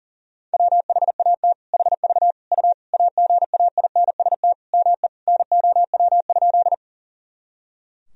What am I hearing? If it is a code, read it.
Morse code, heard as a keyed tone, about 39 wpm, 700 Hz. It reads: WHAT HV U AGAINST ME NOW?
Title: Sentences from Top 400 Words and Abbreviations 40wpm